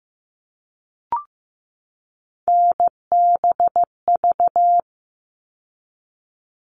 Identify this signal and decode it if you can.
Morse code, heard as a keyed tone, about 15 wpm, 700 Hz. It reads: NBV
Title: All Cut Numbers - 3-Digit 15wpm